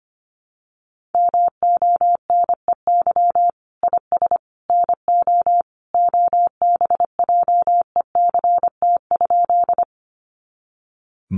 Morse code, heard as a keyed tone, about 25 wpm, 700 Hz. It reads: MONEY IS NO OBJECT?
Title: Sentences from Top 500 Words 25wpm